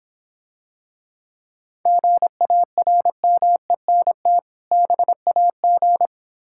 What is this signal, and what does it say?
Morse code, heard as a keyed tone, about 26 wpm, 700 Hz. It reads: GARMENT BAG